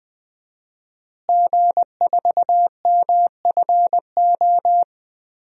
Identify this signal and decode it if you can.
Morse code, heard as a keyed tone, about 20 wpm, 700 Hz. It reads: G4MFO
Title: DXCC-100 Call Signs - 5 Characters - Repeated 2x - 20wpm